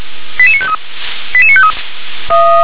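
Five-tone selective call (ZVEI), heard as quick runs of five tones, tones of about 70 ms. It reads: 88953 88953
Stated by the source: ZVEI five-tone selective call received on an SDR.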